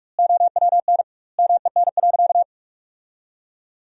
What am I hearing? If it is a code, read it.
Morse code, heard as a keyed tone, about 42 wpm, 700 Hz. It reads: OWN MEN.
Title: 2x Word-Spacing - Sentences from Top 300 Words - 45wpm